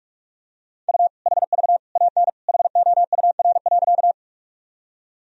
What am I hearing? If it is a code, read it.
Morse code, heard as a keyed tone, about 41 wpm, 700 Hz. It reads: U HV AN HOUR.
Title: Sentences from Top 400 Words and Abbreviations 45wpm